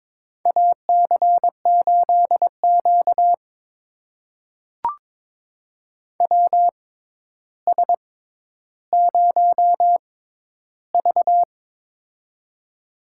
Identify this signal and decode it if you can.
Morse code, heard as a keyed tone, about 21 wpm, spaced as for 11 wpm, 700 Hz. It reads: AC8Q WS0V
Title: USA Call Signs - 4 Characters - 6x Character Spacing - 22wpm